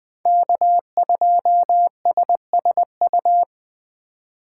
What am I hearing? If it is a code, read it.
Morse code, heard as a keyed tone, about 23 wpm, 700 Hz. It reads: K2SSU
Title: Speed Racing - USA 1x3 Call Signs 20wpm